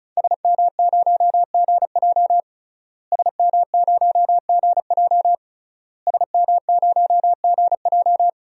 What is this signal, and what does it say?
Morse code, heard as a keyed tone, about 34 wpm, 700 Hz. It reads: SM0GJ SM0GJ SM0GJ
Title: DXCC-100 Call Signs - 5 Characters - Repeated 3x - 35wpm